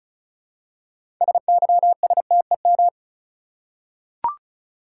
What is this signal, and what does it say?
Morse code, heard as a keyed tone, about 36 wpm, 700 Hz. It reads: SYSTEM
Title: Six Letter Words - 35wpm